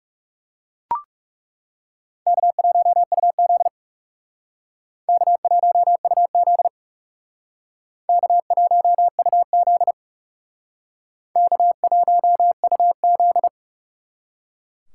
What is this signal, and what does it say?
Morse code, heard as a keyed tone, about 38 wpm, 700 Hz. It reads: K1UZ K1UZ K1UZ K1UZ